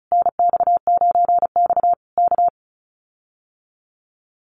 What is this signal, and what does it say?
Morse code, heard as a keyed tone, about 34 wpm, 700 Hz. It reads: NX9X K
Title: Common QSO Phrases Encore 35wpm